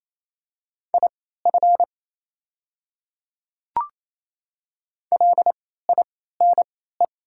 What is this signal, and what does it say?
Morse code, heard as a keyed tone, about 28 wpm, spaced as for 16 wpm, 700 Hz. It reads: IF LINE
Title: 3x Character Spacing - Top 300 Words - 28wpm